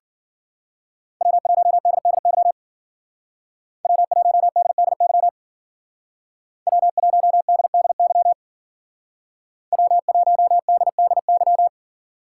Morse code, heard as a keyed tone, about 55 wpm, 700 Hz. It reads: W1DDY W1DDY W1DDY W1DDY